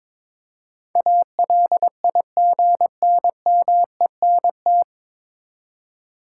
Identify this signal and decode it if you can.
Morse code, heard as a keyed tone, about 22 wpm, 700 Hz. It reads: ALIGNMENT
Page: Morse Code Ninja practice 22wpm